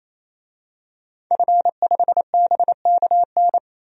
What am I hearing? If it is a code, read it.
Morse code, heard as a keyed tone, about 28 wpm, 700 Hz. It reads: F5BKN
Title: DXCC-100 Call Signs - 5 Characters - 28wpm